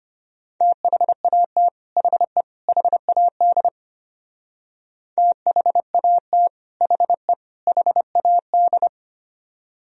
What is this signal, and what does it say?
Morse code, heard as a keyed tone, about 31 wpm, 700 Hz. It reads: THAT HE HAD THAT HE HAD
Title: Speed-Racing - Sets of 3 Words 25wpm